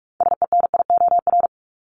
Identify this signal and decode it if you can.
Morse code, heard as a keyed tone, about 44 wpm, 700 Hz. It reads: SENIOR
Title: Top 1751-2000 Words 45wpm